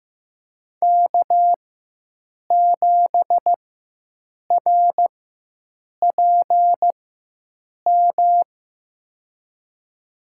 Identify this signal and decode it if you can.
Morse code, heard as a keyed tone, about 15 wpm, spaced as for 7 wpm, 700 Hz. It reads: K7RPM
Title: USA Call Signs - 4x Character Spacing - 15wpm